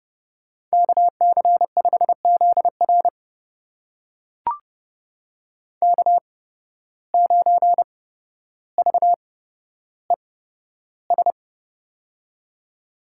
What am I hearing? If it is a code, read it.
Morse code, heard as a keyed tone, about 30 wpm, spaced as for 13 wpm, 700 Hz. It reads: KC5ZR K9VES
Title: USA Call Signs - 5 Characters - 8x Character Spacing - 30wpm